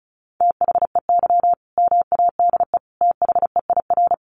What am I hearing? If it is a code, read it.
Morse code, heard as a keyed tone, about 34 wpm, 700 Hz. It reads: THEY MADE THEIR
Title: Sentences from Top 1100 Words 35wpm